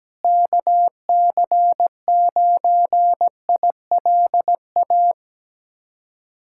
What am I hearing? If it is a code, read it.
Morse code, heard as a keyed tone, about 17 wpm, 700 Hz. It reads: KC9ILA
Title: USA Callsign DE Callsign 17wpm